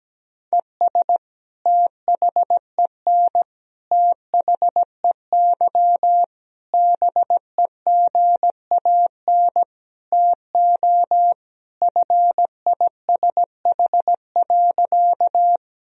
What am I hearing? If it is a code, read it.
Morse code, heard as a keyed tone, about 17 wpm, 700 Hz. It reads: ES THEN THEY BEGAN TO FISH.
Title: Sentences from Top 300 Words and Abbreviations 17wpm